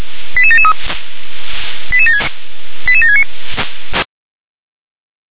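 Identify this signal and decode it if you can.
Five-tone selective call (ZVEI), heard as quick runs of five tones, tones of about 70 ms. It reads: 88773 88769
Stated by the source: ZVEI five-tone selective call received on SDR.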